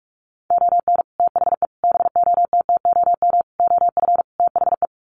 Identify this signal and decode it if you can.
Morse code, heard as a keyed tone, about 44 wpm, 700 Hz. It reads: ON THE BOTTOM OF THE
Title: Sets of 5 Words Encore2 45wpm